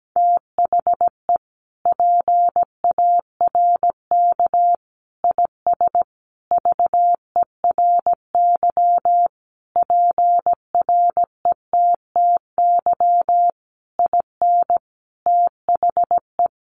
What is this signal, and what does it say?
Morse code, heard as a keyed tone, about 17 wpm, 700 Hz. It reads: THE PARK IS VERY PRETTY IN THE
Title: Sentences from Top 1300 Words 17wpm